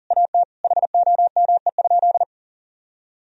Morse code, heard as a keyed tone, about 37 wpm, 700 Hz. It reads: AT HOME?
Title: Sentences from Top 1000 Words - Review Part 4 - 40wpm